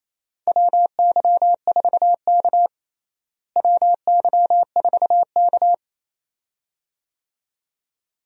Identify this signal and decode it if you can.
Morse code, heard as a keyed tone, about 28 wpm, 700 Hz. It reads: WY4K WY4K
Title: USA Call Signs - 4 Characters - Repeated 2x - 28wpm